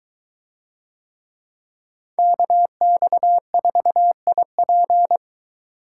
Morse code, heard as a keyed tone, about 21 wpm, 700 Hz. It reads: KX4IP